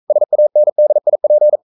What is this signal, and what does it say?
Morse code, heard as a keyed tone, about 42 wpm, 600 Hz. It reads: SANDIP
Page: Romanen Never af Ken Follet på engelsk.